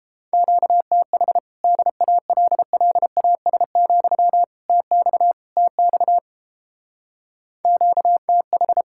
Morse code, heard as a keyed tone, about 30 wpm, 700 Hz. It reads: QTH DALLAS, TX TX QTH